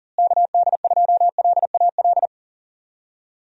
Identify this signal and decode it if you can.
Morse code, heard as a keyed tone, about 39 wpm, 700 Hz. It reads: KD2LAL
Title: USA Call Signs - 6 Characters - Repeated 2x - 40wpm